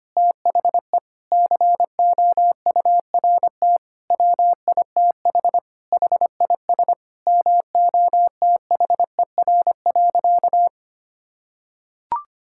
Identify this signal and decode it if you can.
Morse code, heard as a keyed tone, about 25 wpm, 700 Hz. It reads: THE COURT WITH HIS MOTHER.
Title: Sentences from Top 1600 Words 25wpm